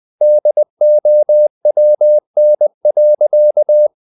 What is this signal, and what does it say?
Morse code, heard as a keyed tone, about 20 wpm, 600 Hz. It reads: DOWN.
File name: the old man and the sea / the old man and the sea part 1 20wpm